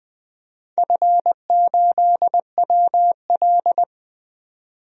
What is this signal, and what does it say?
Morse code, heard as a keyed tone, about 20 wpm, 700 Hz. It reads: F8WL
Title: DXCC-100 Call Signs - 4 Characters - 20wpm